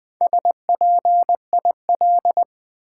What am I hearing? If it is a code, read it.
Morse code, heard as a keyed tone, about 20 wpm, 700 Hz. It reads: SPIL
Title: Build A Word - 1 Letter At A Time - 5-Letter Words - 20wpm